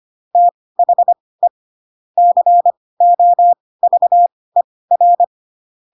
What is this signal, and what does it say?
Morse code, heard as a keyed tone, about 25 wpm, spaced as for 18 wpm, 700 Hz. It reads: THE COVER